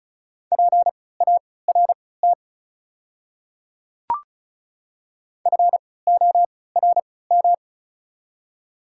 Morse code, heard as a keyed tone, about 34 wpm, spaced as for 20 wpm, 700 Hz. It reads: PART FORM
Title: 3x Character Spacing - Top 200 Words - 35wpm